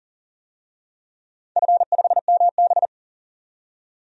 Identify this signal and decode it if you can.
Morse code, heard as a keyed tone, about 40 wpm, 700 Hz. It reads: F5MB
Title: DXCC-100 Call Signs - 4 Characters - 40wpm